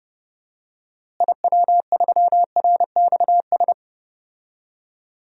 Morse code, heard as a keyed tone, about 30 wpm, 700 Hz. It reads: IW3RXS
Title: DXCC-100 Call Signs - 6 Characters - Repeated 3x - 30wpm